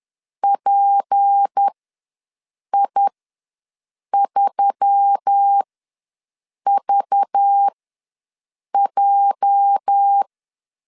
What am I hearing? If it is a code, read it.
Morse code, heard as a keyed tone, about 10 wpm, spaced as for 6 wpm, 800 Hz. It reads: PI3VJ
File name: LEZIONI / 8 YJ45 / 8 YJ ws 10wpm 4043